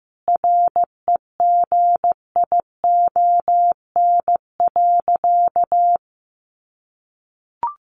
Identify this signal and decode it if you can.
Morse code, heard as a keyed tone, about 15 wpm, 700 Hz. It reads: REGION.
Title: Sentences from Top 1500 Words 15wpm